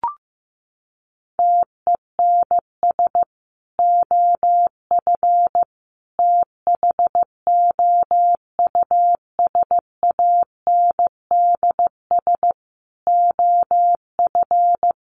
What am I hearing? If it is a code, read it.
Morse code, heard as a keyed tone, about 15 wpm, 700 Hz. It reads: TENS OF THOUSANDS OF
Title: Sets of 5 Words Encore2 15wpm